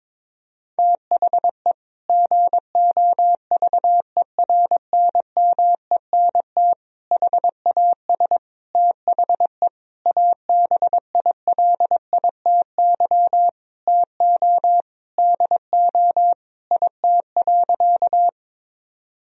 Morse code, heard as a keyed tone, about 22 wpm, 700 Hz. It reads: THE GOVERNMENT HAS THE ABILITY TO DO IT.
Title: Sentences from Top 1000 Words - Review Part 3 - 22wpm